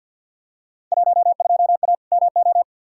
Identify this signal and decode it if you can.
Morse code, heard as a keyed tone, about 52 wpm, 700 Hz. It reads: 12A MO
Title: Field Day Exchanges Encore - 50wpm